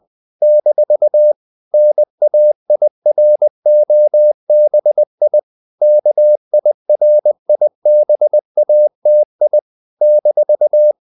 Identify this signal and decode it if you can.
Morse code, heard as a keyed tone, about 20 wpm, 600 Hz. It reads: - NAIROBI KIRIBATI -
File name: CW / Practice / NationalCapitals20